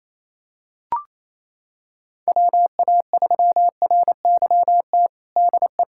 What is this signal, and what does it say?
Morse code, heard as a keyed tone, about 28 wpm, 700 Hz. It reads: WA3RYT DE
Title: USA Callsign DE Callsign 28wpm